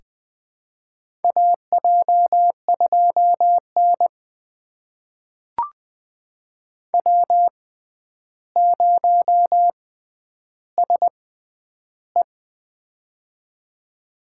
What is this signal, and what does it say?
Morse code, heard as a keyed tone, about 20 wpm, spaced as for 10 wpm, 700 Hz. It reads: AJ2N W0SE
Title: USA Call Signs - 4 Characters - 6x Character Spacing - 20wpm